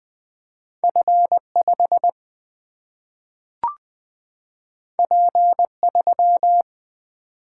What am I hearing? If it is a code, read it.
Morse code, heard as a keyed tone, about 20 wpm, 700 Hz. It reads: F 5 P 3
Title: Single-Letter Single-Number 20wpm